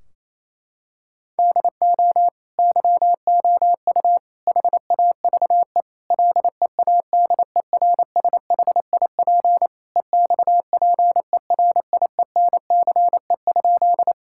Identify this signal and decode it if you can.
Morse code, heard as a keyed tone, about 28 wpm, 700 Hz. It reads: DO YOU HAVE LEADERSHIP EXPERIENCE?
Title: Sentences from Top 1600 Words 28wpm